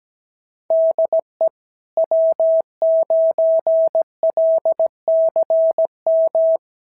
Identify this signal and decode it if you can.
Morse code, heard as a keyed tone, about 17 wpm, 650 Hz. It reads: DE W9LCM
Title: SOTA QSO Practice 17wpm